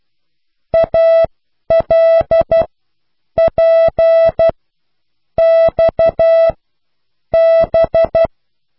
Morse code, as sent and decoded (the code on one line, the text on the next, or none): .- .-.. .--. -..- -...
ALPXB